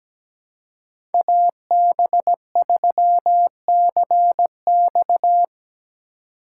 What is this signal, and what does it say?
Morse code, heard as a keyed tone, about 17 wpm, 700 Hz. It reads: AB3CX
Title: USA Call Signs - 2x Character Spacing - 17wpm